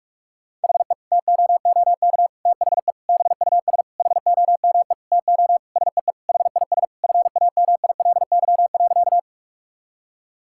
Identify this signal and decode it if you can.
Morse code, heard as a keyed tone, about 46 wpm, 700 Hz. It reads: HE TOOK THE BUS HOME TO SEE HIS FAMILY.
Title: Sentences from Top 1000 Words - Review Part 6 - 45wpm